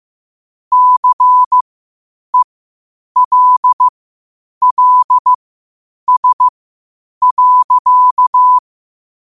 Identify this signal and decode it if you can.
Morse code, heard as a keyed tone, about 15 wpm, spaced as for 8 wpm, 1000 Hz. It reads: CELLS.